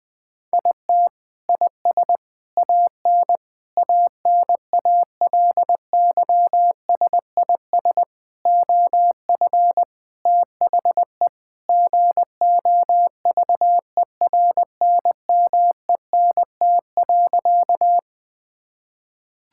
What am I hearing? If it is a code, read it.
Morse code, heard as a keyed tone, about 20 wpm, 700 Hz. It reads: IT IS AN ANALYSIS OF THE GOVERNMENT.
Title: Sentences from Top 1000 Words - Review Part 5 - 20wpm